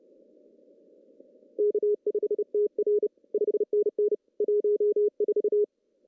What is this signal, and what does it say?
Morse code, heard as a keyed tone, about 30 wpm, 410 Hz. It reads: K5TR 5NN 14
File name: CW / cqww-cw-k5tr